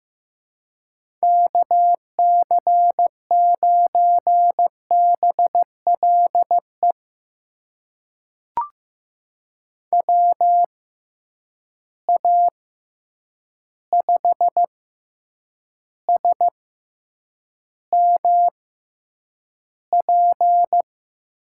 Morse code, heard as a keyed tone, about 15 wpm, spaced as for 8 wpm, 700 Hz. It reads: KC9BLE WA5SMP
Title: USA Call Signs - 6 Characters - 6x Character Spacing - 15wpm